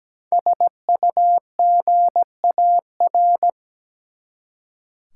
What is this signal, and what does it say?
Morse code, heard as a keyed tone, about 17 wpm, 700 Hz. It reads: SUGAR
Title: Five Letter Words - 17wpm